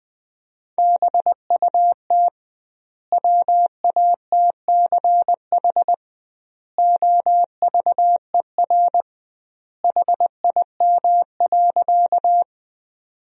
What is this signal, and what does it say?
Morse code, heard as a keyed tone, about 20 wpm, 700 Hz. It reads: BUT WATCH OVER HIM.
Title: 2x Word-Spacing - Sentences from Top 300 Words - 20wpm